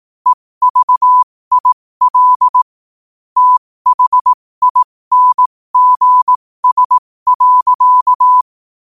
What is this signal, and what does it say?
Morse code, heard as a keyed tone, about 18 wpm, spaced as for 15 wpm, 1000 Hz. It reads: EVIL THINGS.
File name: qotdfiles / 15wpm